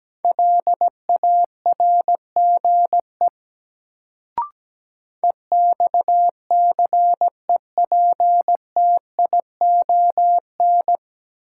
Morse code, heard as a keyed tone, about 17 wpm, 700 Hz. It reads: LARGE EXCEPTION
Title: Sets of 4 Words - New Word Drill - 17wpm